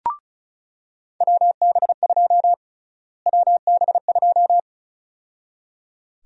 Morse code, heard as a keyed tone, about 36 wpm, 700 Hz. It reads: WB2 WB2
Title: USA Call Sign Prefixes - Repeated 2x - 35wpm